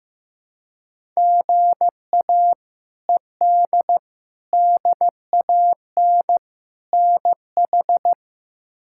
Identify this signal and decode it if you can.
Morse code, heard as a keyed tone, about 14 wpm, 700 Hz. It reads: GA ED DAN NH
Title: K1USN SST QSO Practice - 15wpm